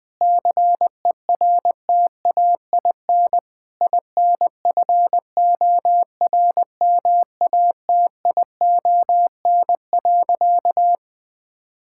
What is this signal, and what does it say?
Morse code, heard as a keyed tone, about 20 wpm, 700 Hz. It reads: CERTAIN INFORMATION.
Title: Sentences from Top 1000 Words - Review Part 6 - 20wpm